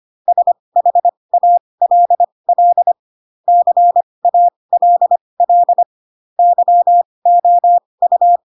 Morse code, heard as a keyed tone, about 25 wpm, spaced as for 20 wpm, 700 Hz. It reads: SHALL CALL YOU
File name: Pinocchio / Pinocchio ch2 15 25 ik1hge